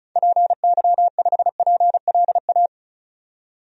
Morse code, heard as a keyed tone, about 34 wpm, 700 Hz. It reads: PY5PLA
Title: DXCC-100 Call Signs - 6 Characters - 35wpm